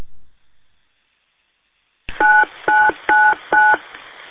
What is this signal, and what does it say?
DTMF, heard as telephone touch tones, tones of about 240 ms. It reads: 9999
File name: qsorec SimplexLogic